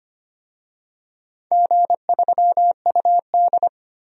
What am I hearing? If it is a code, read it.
Morse code, heard as a keyed tone, about 25 wpm, 700 Hz. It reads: G3UD
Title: DXCC-100 Call Signs - 4 Characters - 25wpm